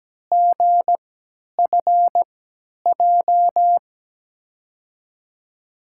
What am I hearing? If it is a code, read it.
Morse code, heard as a keyed tone, about 17 wpm, spaced as for 10 wpm, 700 Hz. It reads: GFJ
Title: USA Call Signs - 6 Characters - 3x Character Spacing - 17wpm